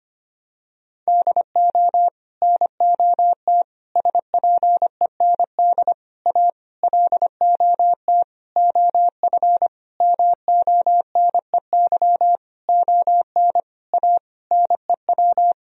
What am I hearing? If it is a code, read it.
Morse code, heard as a keyed tone, about 25 wpm, 700 Hz. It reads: DO NOT SPEND A LOT OF MONEY ON A NEW
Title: Sentences from Top 1000 Words - Review Part 4 - 25wpm